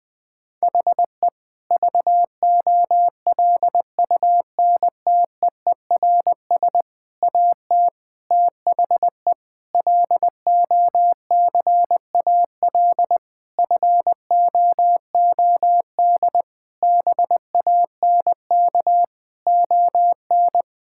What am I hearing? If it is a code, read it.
Morse code, heard as a keyed tone, about 20 wpm, 700 Hz. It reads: HE VOLUNTEERS AT THE LOCAL FOOD BANK ON
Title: Difficult Sentences with Hints - 20wpm